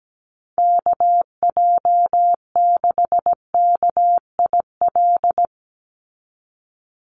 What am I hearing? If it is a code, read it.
Morse code, heard as a keyed tone, about 17 wpm, 700 Hz. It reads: KJ6KIL
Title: USA Call Signs 17wpm